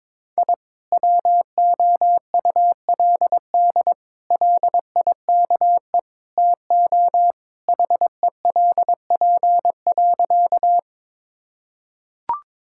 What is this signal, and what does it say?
Morse code, heard as a keyed tone, about 21 wpm, 700 Hz. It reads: I WOULD LIKE TO HELP.
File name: practice / 22wpm / Sentences with 4-Word Phrases - 22wpm